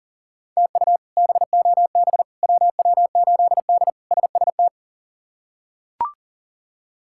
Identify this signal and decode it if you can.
Morse code, heard as a keyed tone, about 37 wpm, 700 Hz. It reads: TU BOB WW8D SST
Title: K1USN SST QSO Practice - 40wpm